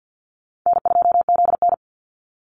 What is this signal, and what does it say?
Morse code, heard as a keyed tone, about 50 wpm, 700 Hz. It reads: N2ZN
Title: Easy Call Signs - 50wpm